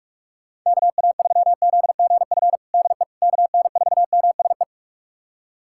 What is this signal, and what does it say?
Morse code, heard as a keyed tone, about 41 wpm, 700 Hz. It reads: KA3ZGF DE KN4MSE
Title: USA Callsign DE Callsign 45wpm